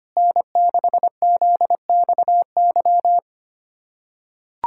Morse code, heard as a keyed tone, about 25 wpm, 700 Hz. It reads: N6ZXY
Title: USA Callsign DE Callsign 25wpm